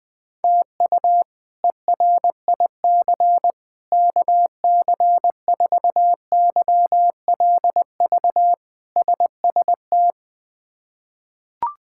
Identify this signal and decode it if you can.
Morse code, heard as a keyed tone, about 21 wpm, 700 Hz. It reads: TU ERIC KC4YLV SST
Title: K1USN SST QSO Practice - 20wpm